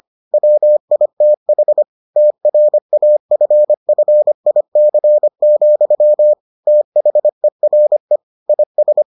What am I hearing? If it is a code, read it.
Morse code, heard as a keyed tone, about 25 wpm, 600 Hz. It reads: WITH TRAFFIC, THERE IS